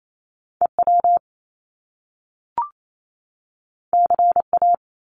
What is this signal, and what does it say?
Morse code, heard as a keyed tone, about 28 wpm, 700 Hz. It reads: EW CA
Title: Top Two Letter Combinations in Words 28wpm